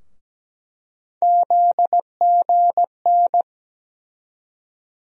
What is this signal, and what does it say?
Morse code, heard as a keyed tone, about 17 wpm, 700 Hz. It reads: ZGN
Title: USA Call Sign Postfixes - 3 Letters 17wpm